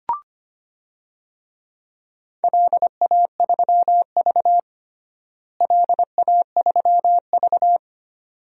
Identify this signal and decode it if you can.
Morse code, heard as a keyed tone, about 25 wpm, 700 Hz. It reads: LA3V LA3V
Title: DXCC-100 Call Signs - 4 Characters - Repeated 2x - 25wpm